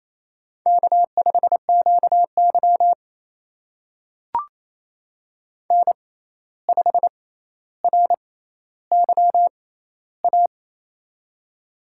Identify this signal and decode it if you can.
Morse code, heard as a keyed tone, about 27 wpm, spaced as for 13 wpm, 700 Hz. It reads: K5QY N5RYA
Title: USA Call Signs - 6x Character Spacing - 28wpm